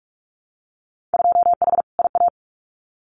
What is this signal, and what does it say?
Morse code, heard as a keyed tone, about 46 wpm, 700 Hz. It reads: 2H IA
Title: Winter Field Day Exchanges Encore 45wpm